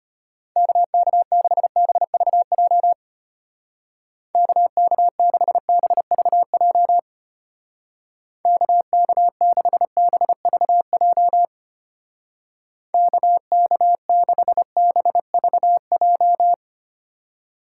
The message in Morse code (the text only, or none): KK6BVJ KK6BVJ KK6BVJ KK6BVJ